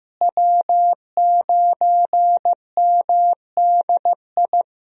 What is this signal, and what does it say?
Morse code, heard as a keyed tone, about 16 wpm, 700 Hz. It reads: W9MDI
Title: USA Call Signs - 5 Characters - Repeated 3x - 15wpm